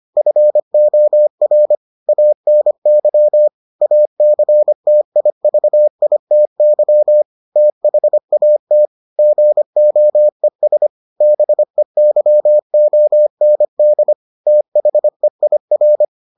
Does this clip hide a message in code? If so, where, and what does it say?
Yes: Morse code from 0.2 to 16.0 s, heard as a keyed tone, about 25 wpm, 600 Hz. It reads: FOR ANY ACTIVITY THAT GOES BEYOND THEIR